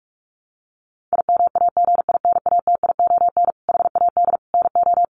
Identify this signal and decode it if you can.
Morse code, heard as a keyed tone, about 44 wpm, 700 Hz. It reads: IMAGINATION HAD NO
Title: Sentences from Top 1300 Words 45wpm